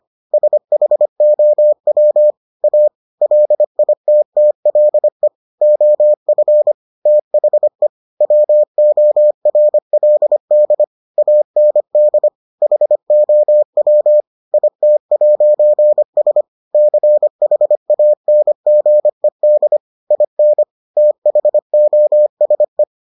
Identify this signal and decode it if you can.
Morse code, heard as a keyed tone, about 25 wpm, 600 Hz. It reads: SHOW A LITTLE OF THE WORLD AND HOW IT'S CHANGED IN THOSE